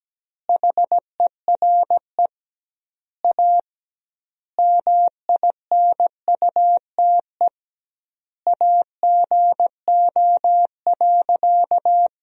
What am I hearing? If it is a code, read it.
Morse code, heard as a keyed tone, about 17 wpm, 700 Hz. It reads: HERE A MINUTE AGO.
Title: 2x Word-Spacing - Sentences from Top 500 Words - 17wpm